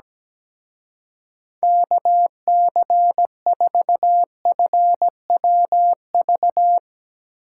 Morse code, heard as a keyed tone, about 17 wpm, 700 Hz. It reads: KC4FWV
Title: Speed Racing - USA Call Signs 17wpm